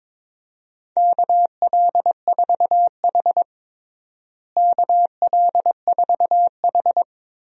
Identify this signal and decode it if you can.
Morse code, heard as a keyed tone, about 22 wpm, 700 Hz. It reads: KL4H KL4H
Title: USA Call Signs - 4 Characters - Repeated 3x - 22wpm